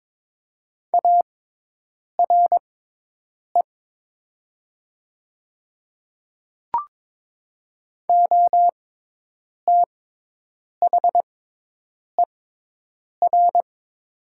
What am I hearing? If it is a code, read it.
Morse code, heard as a keyed tone, about 23 wpm, spaced as for 8 wpm, 700 Hz. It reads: ARE OTHER
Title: 6x Character Spacing - Top 100 Words - 22wpm